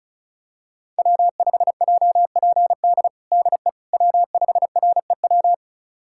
Morse code, heard as a keyed tone, about 36 wpm, 700 Hz. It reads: W5JPD DE W5REW
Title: USA Callsign DE Callsign - 5 Characters - 35wpm